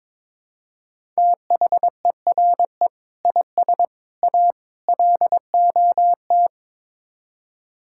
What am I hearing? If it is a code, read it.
Morse code, heard as a keyed tone, about 22 wpm, 700 Hz. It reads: THERE IS A LOT
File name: practice / 22wpm / Build A Phrase - 5-Word Phrases - 22wpm